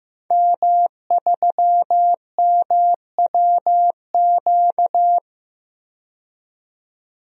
Morse code, heard as a keyed tone, about 16 wpm, 700 Hz. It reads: M3MWQ
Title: DXCC-100 Call Signs - 5 Characters - 15wpm